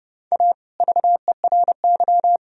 Morse code, heard as a keyed tone, about 38 wpm, 700 Hz. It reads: A VERY